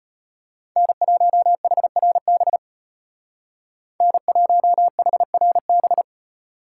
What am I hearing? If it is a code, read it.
Morse code, heard as a keyed tone, about 30 wpm, 700 Hz. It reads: N1HRB N1HRB
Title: Speed Racing - USA 1x3 Call Signs 25wpm